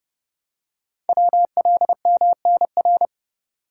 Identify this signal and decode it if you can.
Morse code, heard as a keyed tone, about 30 wpm, 700 Hz. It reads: WLMNR
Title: Code Groups - 5 Letters - 30wpm